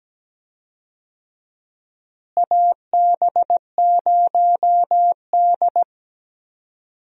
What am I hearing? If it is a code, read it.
Morse code, heard as a keyed tone, about 17 wpm, 700 Hz. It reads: AB0D
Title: USA Call Signs - 4 Characters - Repeated 2x - 17wpm